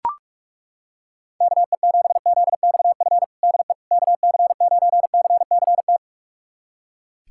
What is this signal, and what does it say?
Morse code, heard as a keyed tone, about 42 wpm, 700 Hz. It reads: KE7ZXF DE KC9CCT
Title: USA Callsign DE Callsign - 6 Characters - 45wpm